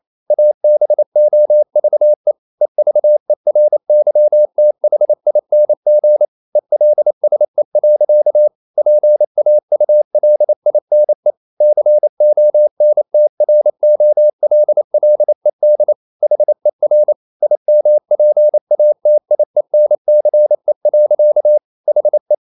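Morse code, heard as a keyed tone, about 28 wpm, 600 Hz. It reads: ABOVE EVERYTHING ELSE. PAULINE CONTROLLED HER IMPATIENCE. HE
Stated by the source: Morse code practice at 28 wpm, text of Ken Follett's novel Never in English, from CW med Gnister.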